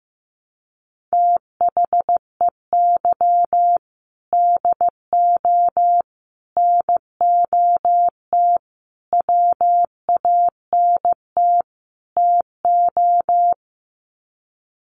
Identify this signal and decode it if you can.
Morse code, heard as a keyed tone, about 15 wpm, 700 Hz. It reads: THEY DO NOT WANT TO
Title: Sets of 5 Words 15wpm